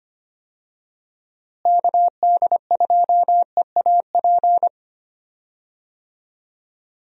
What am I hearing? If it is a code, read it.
Morse code, heard as a keyed tone, about 25 wpm, 700 Hz. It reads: KD2EAP
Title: USA Call Signs - 6 Characters - Repeated 2x - 25wpm